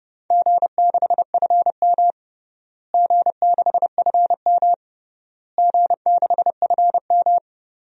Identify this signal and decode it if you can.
Morse code, heard as a keyed tone, about 30 wpm, 700 Hz. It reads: G6FM G6FM G6FM
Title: DXCC-100 Call Signs - 4 Characters - Repeated 3x - 30wpm